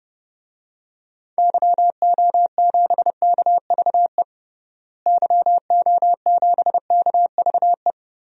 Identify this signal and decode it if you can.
Morse code, heard as a keyed tone, about 30 wpm, 700 Hz. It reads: YO7KVE YO7KVE
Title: DXCC-100 Call Signs - 6 Characters - Repeated 3x - 30wpm